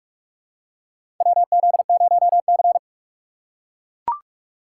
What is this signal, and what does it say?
Morse code, heard as a keyed tone, about 42 wpm, 700 Hz. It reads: WZ0C
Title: USA Call Signs - 4 Characters - 6x Character Spacing - 45wpm